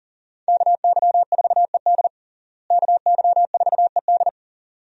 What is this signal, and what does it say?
Morse code, heard as a keyed tone, about 40 wpm, 700 Hz. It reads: KY4ED KY4ED
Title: USA Call Signs - 5 Characters - Repeated 2x - 40wpm